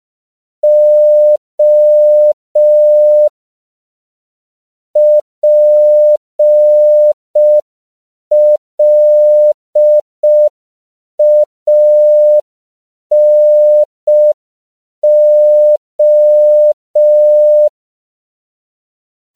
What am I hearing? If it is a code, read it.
Morse code, heard as a keyed tone, about 5 wpm, 600 Hz. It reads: O PLANO
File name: CW / 5PPM